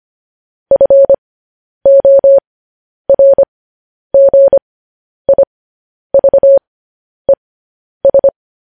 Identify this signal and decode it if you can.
Morse code, heard as a keyed tone, about 25 wpm, spaced as for 10 wpm, 550 Hz. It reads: FORGIVES
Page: news-headlines-in-morse-code-at-10-wpm